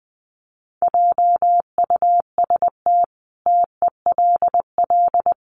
Morse code, heard as a keyed tone, about 20 wpm, 700 Hz. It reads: JUST TELL
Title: Sentences from Top 800 Words 20wpm